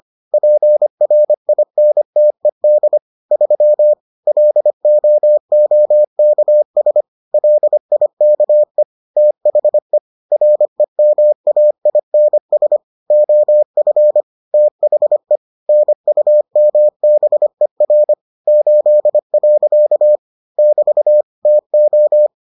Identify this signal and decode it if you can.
Morse code, heard as a keyed tone, about 25 wpm, 600 Hz. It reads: PRINTED 3 LOOKS LIKE THE REMAINS OF THE NUMBER 8. <BT> TO